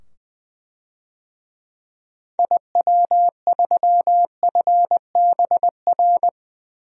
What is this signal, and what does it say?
Morse code, heard as a keyed tone, about 20 wpm, 700 Hz. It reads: IW3FBR
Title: DXCC-100 Call Signs - 6 Characters - 20wpm